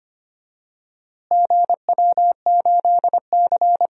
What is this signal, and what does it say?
Morse code, heard as a keyed tone, about 25 wpm, 700 Hz. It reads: GW8C
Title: DXCC-100 Call Signs - 4 Characters - Repeated 2x - 25wpm